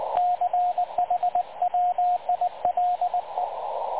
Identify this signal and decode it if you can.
Morse code, heard as a keyed tone, about 19 wpm, 720 Hz, with radio noise. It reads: CHWILE
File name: dlazaawansowanych / nagraniazpasma / websdr recording 3567.0kHz